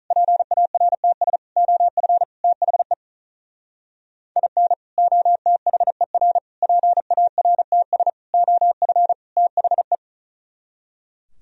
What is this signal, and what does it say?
Morse code, heard as a keyed tone, about 44 wpm, 700 Hz. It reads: PARTS OF THE IN OTHER PARTS OF THE